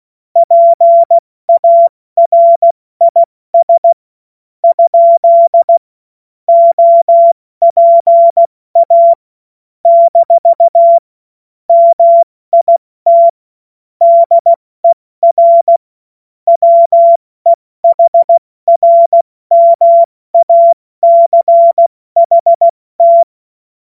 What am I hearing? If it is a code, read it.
Morse code, heard as a keyed tone, about 16 wpm, spaced as for 14 wpm, 680 Hz. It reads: PARIS? OPA - MIT DER WEHRMACHT
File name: Morsekurs / Witze / Witz 19. Oktober 2021, 70 F80